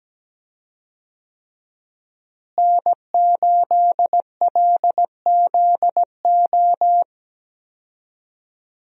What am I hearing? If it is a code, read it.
Morse code, heard as a keyed tone, about 17 wpm, 700 Hz. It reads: N8LZO
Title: USA Call Signs - 5 Characters - Repeated 3x - 17wpm